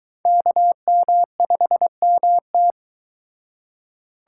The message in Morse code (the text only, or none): KM5MT